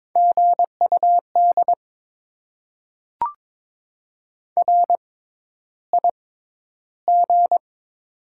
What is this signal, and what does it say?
Morse code, heard as a keyed tone, about 21 wpm, spaced as for 11 wpm, 700 Hz. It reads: GUD RIG
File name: Top 20 QSO Elements - 6x Character Spacing - 22wpm